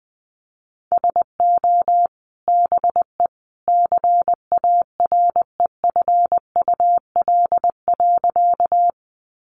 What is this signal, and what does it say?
Morse code, heard as a keyed tone, about 20 wpm, 700 Hz. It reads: SO BE CAREFUL.
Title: A Sound Of Thunder 20wpm